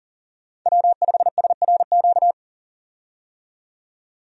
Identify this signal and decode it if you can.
Morse code, heard as a keyed tone, about 40 wpm, 700 Hz. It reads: W5SRQ
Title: USA Call Signs - 5 Characters - 40wpm